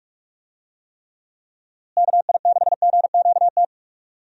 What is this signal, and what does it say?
Morse code, heard as a keyed tone, about 46 wpm, 700 Hz. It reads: KI6GQT
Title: USA Call Signs - 6 Characters - Repeated 2x - 45wpm